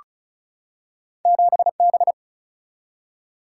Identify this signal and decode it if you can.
Morse code, heard as a keyed tone, about 36 wpm, 700 Hz. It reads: 7B
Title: Single-Letter Single-Number 35wpm